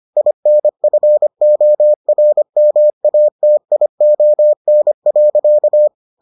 Morse code, heard as a keyed tone, about 25 wpm, 600 Hz. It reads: INFORMATION.